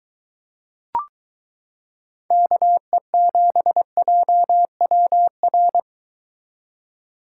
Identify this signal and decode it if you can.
Morse code, heard as a keyed tone, about 22 wpm, 700 Hz. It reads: KE7JWR